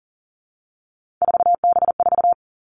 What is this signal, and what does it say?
Morse code, heard as a keyed tone, about 38 wpm, 690 Hz. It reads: 4B4
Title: Three Challenging Letter-Number Combinations 40wpm